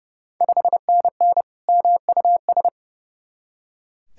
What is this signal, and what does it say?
Morse code, heard as a keyed tone, about 32 wpm, 700 Hz. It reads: 5NN MUS
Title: Ontario QSO Party Exchange 30wpm